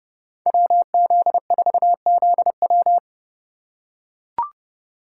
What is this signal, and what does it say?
Morse code, heard as a keyed tone, about 30 wpm, 700 Hz. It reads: WZ4ZW
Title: USA Call Signs - 5 Characters - 2x Character Spacing - 30wpm